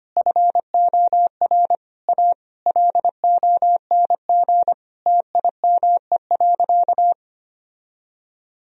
Morse code, heard as a keyed tone, about 25 wpm, 700 Hz. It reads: FOR A LONG TIME.
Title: Sentences from Top 1000 Words - Review Part 1 - 25wpm